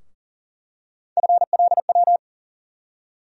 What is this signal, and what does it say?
Morse code, heard as a keyed tone, about 40 wpm, 700 Hz. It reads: FLW
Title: USA Call Sign Postfixes - 3 Letters 40wpm